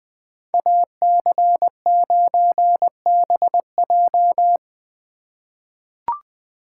Morse code, heard as a keyed tone, about 21 wpm, 700 Hz. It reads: AC9BJ
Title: USA Call Signs - 5 Characters - 3x Character Spacing - 20wpm